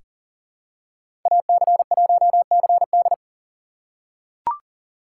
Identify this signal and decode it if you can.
Morse code, heard as a keyed tone, about 40 wpm, 700 Hz. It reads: AC1CD
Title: USA Call Signs - 6x Character Spacing - 40wpm